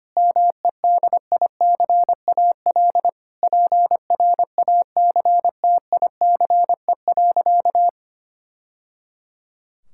Morse code, heard as a keyed tone, about 25 wpm, 700 Hz. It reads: MEDICAL PRACTICE.
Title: Sentences from Top 1000 Words - Review Part 5 - 25wpm